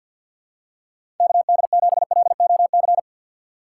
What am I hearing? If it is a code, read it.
Morse code, heard as a keyed tone, about 53 wpm, 700 Hz. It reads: KD7LOC
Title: USA Call Signs - 6 Characters - 3x Character Spacing - 50wpm